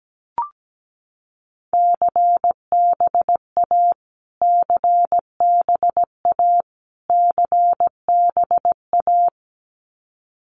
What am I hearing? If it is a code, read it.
Morse code, heard as a keyed tone, about 17 wpm, 700 Hz. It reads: CBA CBA CBA
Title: Abbreviations Repeated 3x - 17wpm